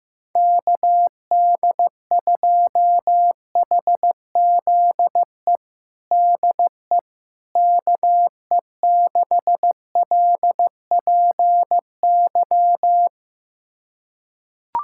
Morse code, heard as a keyed tone, about 16 wpm, 700 Hz. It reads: KD2HZE DE KE6LPY
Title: USA Callsign DE Callsign - 6 Characters - 15wpm